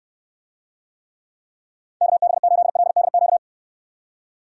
Morse code, heard as a keyed tone, about 67 wpm, 700 Hz. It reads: KB8LDQ